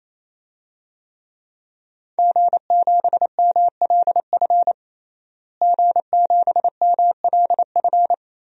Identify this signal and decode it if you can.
Morse code, heard as a keyed tone, about 28 wpm, 700 Hz. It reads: G7MLF G7MLF
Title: DXCC-100 Call Signs - 5 Characters - Repeated 3x - 28wpm